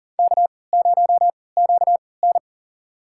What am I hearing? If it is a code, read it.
Morse code, heard as a keyed tone, about 40 wpm, spaced as for 23 wpm, 700 Hz. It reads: K0QN
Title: USA Call Signs - 4 Characters - 3x Character Spacing - 40wpm